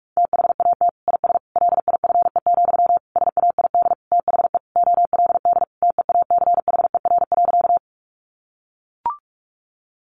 Morse code, heard as a keyed tone, about 44 wpm, 700 Hz. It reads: THAT IS LIFE, SAID THE OLD TEACHER.
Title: Sentences from Top 900 Words 45wpm